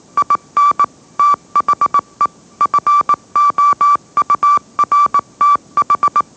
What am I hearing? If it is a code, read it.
Morse code, heard as a keyed tone, about 18 wpm, 1160 Hz. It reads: IN THE FOURTH
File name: mrx / mrx081 20wpm